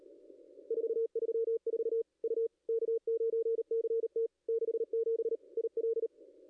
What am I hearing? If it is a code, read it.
Morse code, heard as a keyed tone, about 37 wpm, 440 Hz. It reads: 434 U K9CT 67 IL
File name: contests / ss-cw-k5tr-op-n5rz -0100